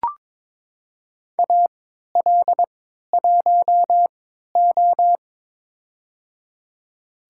Morse code, heard as a keyed tone, about 22 wpm, spaced as for 13 wpm, 700 Hz. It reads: AL1O
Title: USA Call Signs - 4 Characters - 3x Character Spacing - 22wpm